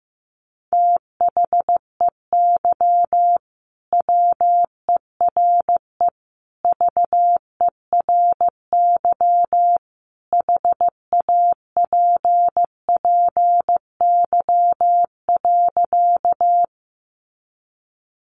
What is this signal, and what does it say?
Morse code, heard as a keyed tone, about 15 wpm, 700 Hz. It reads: THEY WERE VERY HAPPY.
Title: Sentences from Top 1000 Words 15wpm